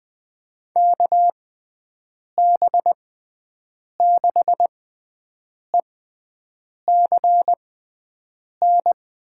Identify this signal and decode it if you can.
Morse code, heard as a keyed tone, about 20 wpm, spaced as for 7 wpm, 700 Hz. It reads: KB6ECN
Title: USA Call Signs - 6 Characters - 6x Character Spacing - 20wpm